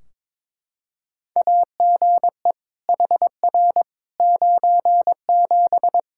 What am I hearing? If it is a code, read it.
Morse code, heard as a keyed tone, about 22 wpm, 700 Hz. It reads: AGE HR 97
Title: Practice QSO - Exchange 3 - More Info - 22wpm